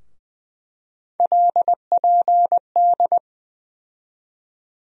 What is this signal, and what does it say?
Morse code, heard as a keyed tone, about 20 wpm, 700 Hz. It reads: LPD
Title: USA Call Sign Postfixes - 3 Letters 20wpm